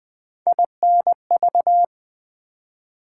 Morse code, heard as a keyed tone, about 20 wpm, 700 Hz. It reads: INV
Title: Build A Word - 3 Letters At A Time - 8-Letter Words - 20wpm